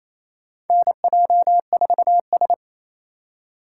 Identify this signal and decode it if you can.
Morse code, heard as a keyed tone, about 28 wpm, 700 Hz. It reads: NJ4S
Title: USA Call Signs - 4 Characters - 2x Character Spacing - 28wpm